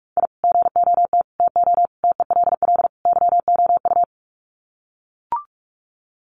Morse code, heard as a keyed tone, about 44 wpm, 700 Hz. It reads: I GOT TO TELL YOU
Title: Sets of 5 Words Encore2 45wpm